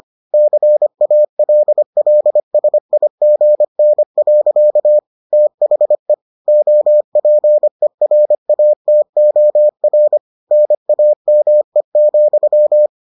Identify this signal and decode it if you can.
Morse code, heard as a keyed tone, about 25 wpm, 600 Hz. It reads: CALLSIGN. THE OPERATOR NAME,